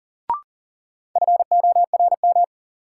Morse code, heard as a keyed tone, about 40 wpm, 700 Hz. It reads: FORM
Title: Sentences from Top 1000 Words - Review Part 1 - New Word Drill - 40wpm